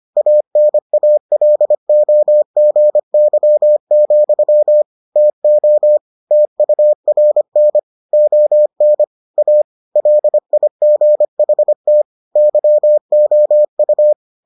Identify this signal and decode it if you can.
Morse code, heard as a keyed tone, about 25 wpm, 600 Hz. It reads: ANALOGY, TO TURN ON A LIGHT YOU